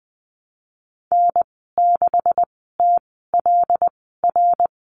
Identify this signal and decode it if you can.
Morse code, heard as a keyed tone, about 20 wpm, spaced as for 15 wpm, 700 Hz. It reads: N6TLR
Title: 140 Lesson 23g - Letter f - Call Signs 20wpm